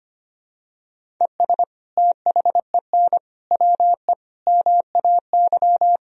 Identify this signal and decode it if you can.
Morse code, heard as a keyed tone, about 25 wpm, 700 Hz. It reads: ES THEN WE MAY
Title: Sentences from Top 300 Words and Abbreviations 25wpm